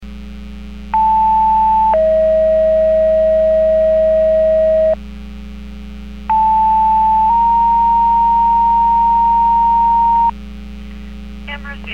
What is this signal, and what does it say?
Two-tone page, heard as a one-second tone, then a three-second tone, about 880 then 640 Hz, about 880 then 930 Hz, with speech. It is heard from 0.9 to 10.3 s.